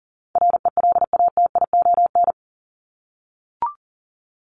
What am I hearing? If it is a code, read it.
Morse code, heard as a keyed tone, about 40 wpm, 700 Hz. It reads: RELATION
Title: ES:S Suffix - Top 100 Words with Base Words 40wpm